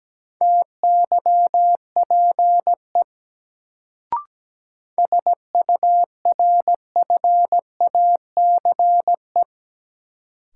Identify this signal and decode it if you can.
Morse code, heard as a keyed tone, about 17 wpm, 700 Hz. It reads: TYPE SURFACE
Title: Sentences from Top 600 Words - New Word Drill - 17wpm